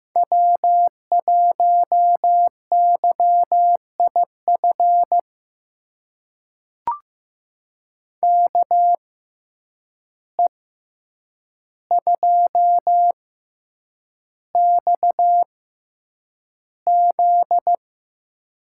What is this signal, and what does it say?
Morse code, heard as a keyed tone, about 15 wpm, spaced as for 8 wpm, 700 Hz. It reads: W1YIF KE2XZ
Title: USA Call Signs - 6x Character Spacing - 15wpm